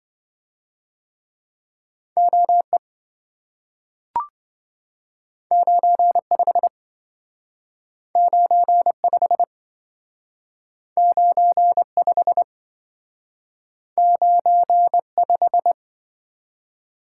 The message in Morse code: OE 95 95 95 95

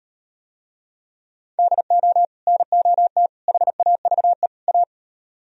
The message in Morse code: DO NOT HAVE A

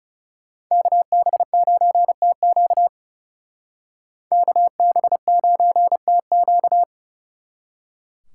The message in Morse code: KB9TQ KB9TQ

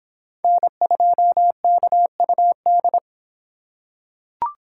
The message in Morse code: N2KUD

